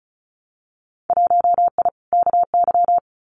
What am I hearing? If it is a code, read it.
Morse code, heard as a keyed tone, about 34 wpm, 700 Hz. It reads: 1I KY
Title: Winter Field Day Exchanges 35wpm